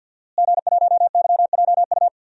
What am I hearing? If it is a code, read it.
Morse code, heard as a keyed tone, about 46 wpm, 700 Hz. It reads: G1YJU
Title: DXCC-100 Call Signs - 5 Characters - Repeated 3x - 50wpm